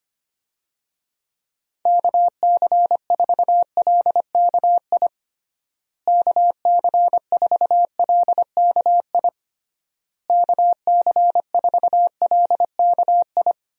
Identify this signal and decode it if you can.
Morse code, heard as a keyed tone, about 25 wpm, 700 Hz. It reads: KC4LKI KC4LKI KC4LKI
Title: USA Call Signs - 6 Characters - Repeated 3x - 25wpm